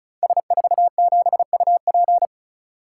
Morse code, heard as a keyed tone, about 36 wpm, 700 Hz. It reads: S47UP
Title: Code Groups - 5 Letter:Numbers - 35wpm